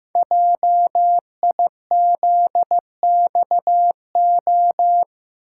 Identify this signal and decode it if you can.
Morse code, heard as a keyed tone, about 15 wpm, 700 Hz. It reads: JIZXO
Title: Code Groups - 5 Letters - 15wpm